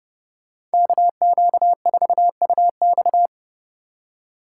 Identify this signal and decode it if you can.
Morse code, heard as a keyed tone, about 32 wpm, 700 Hz. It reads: KQ4UX